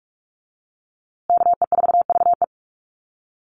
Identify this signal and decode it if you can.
Morse code, heard as a keyed tone, about 44 wpm, 690 Hz. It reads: KE4VE
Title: USA Call Signs Repeated 3x 45wpm